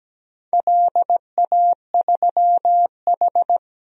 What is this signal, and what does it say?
Morse code, heard as a keyed tone, about 17 wpm, 700 Hz. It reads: LA3H
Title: DXCC-100 Call Signs - 4 Characters - Repeated 3x - 17wpm